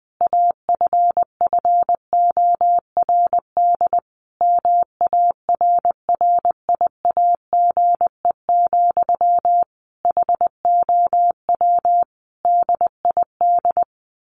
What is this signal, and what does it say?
Morse code, heard as a keyed tone, about 20 wpm, 700 Hz. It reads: AFFORD MARRIAGE, HOW DID
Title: Sentences from Top 1100 Words 20wpm